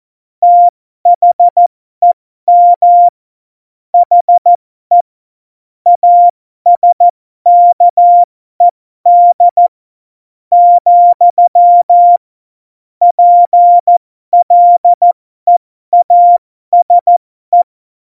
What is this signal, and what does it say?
Morse code, heard as a keyed tone, about 14 wpm, spaced as for 12 wpm, 700 Hz. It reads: THEM HE ASKED, PLEASE
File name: Morsekurs / Witze